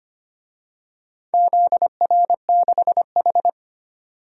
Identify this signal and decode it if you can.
Morse code, heard as a keyed tone, about 25 wpm, 700 Hz. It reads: ZR6H
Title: DXCC-100 Call Signs - 4 Characters - 25wpm